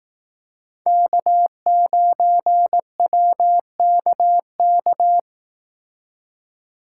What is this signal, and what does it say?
Morse code, heard as a keyed tone, about 19 wpm, 700 Hz. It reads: K9WKK
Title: Speed Racing - USA Call Signs 15wpm